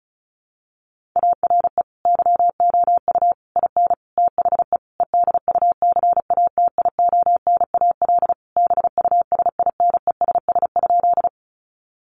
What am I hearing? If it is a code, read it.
Morse code, heard as a keyed tone, about 34 wpm, 700 Hz. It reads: ARE YOU IN THE EDUCATIONAL BUSINESS?
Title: Sentences from Top 1200 Words 35wpm